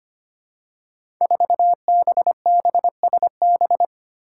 Morse code, heard as a keyed tone, about 25 wpm, 700 Hz. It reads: 4BBSB
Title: Five Challenging Letter-Number Combinations 25wpm